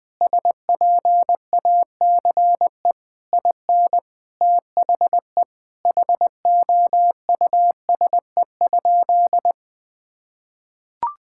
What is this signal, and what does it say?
Morse code, heard as a keyed tone, about 20 wpm, 700 Hz. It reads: SPACE IN THE HOUSE?
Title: Sentences from Top 1000 Words - Review Part 5 - 20wpm